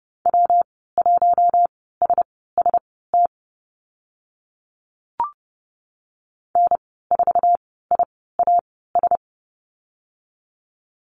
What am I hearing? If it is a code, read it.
Morse code, heard as a keyed tone, about 30 wpm, spaced as for 17 wpm, 700 Hz. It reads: W1SST N4IAS
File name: Lesson 19g - Letter w - Call Signs 30wpm